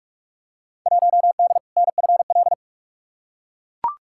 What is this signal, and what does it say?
Morse code, heard as a keyed tone, about 42 wpm, 700 Hz. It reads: 1D NFL
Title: Field Day Exchanges - 45wpm